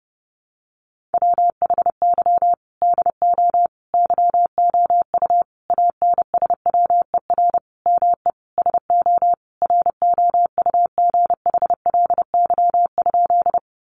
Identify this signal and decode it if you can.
Morse code, heard as a keyed tone, about 30 wpm, 700 Hz. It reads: WHY DO YOU ANSWER ME SO ROUGHLY?
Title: Sentences from Top 1500 Words 30wpm